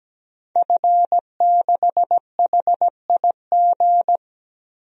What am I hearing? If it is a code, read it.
Morse code, heard as a keyed tone, about 17 wpm, 700 Hz. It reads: F6HIG
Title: DXCC-100 Call Signs - 5 Characters - 17wpm